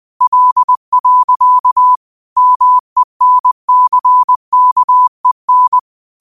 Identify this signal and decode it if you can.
Morse code, heard as a keyed tone, about 20 wpm, 1000 Hz. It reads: L. MENCKEN
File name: qotdfiles / 20wpm